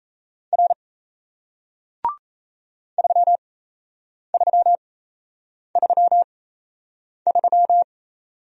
Morse code, heard as a keyed tone, about 38 wpm, spaced as for 8 wpm, 700 Hz. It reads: R 3333